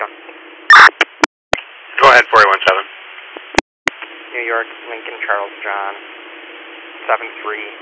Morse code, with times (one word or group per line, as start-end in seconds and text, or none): none